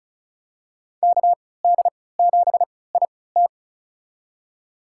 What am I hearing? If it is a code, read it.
Morse code, heard as a keyed tone, about 34 wpm, spaced as for 20 wpm, 700 Hz. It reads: KD7IT
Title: USA Call Signs - 5 Characters - 3x Character Spacing - 35wpm